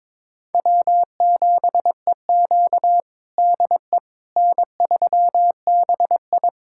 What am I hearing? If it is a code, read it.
Morse code, heard as a keyed tone, about 22 wpm, 700 Hz. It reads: W7EQ DE N3BI
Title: USA Callsign DE Callsign - 4 Characters - 22wpm